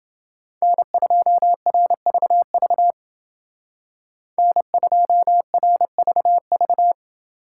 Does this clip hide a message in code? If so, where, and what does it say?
Yes: Morse code from 0.6 to 6.9 s, heard as a keyed tone, about 28 wpm, 700 Hz. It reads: N2RVV N2RVV